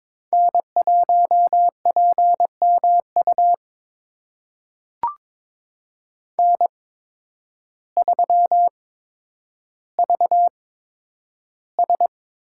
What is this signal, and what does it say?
Morse code, heard as a keyed tone, about 22 wpm, spaced as for 10 wpm, 700 Hz. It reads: N1PMU N3VS
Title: USA Call Signs - 5 Characters - 8x Character Spacing - 22wpm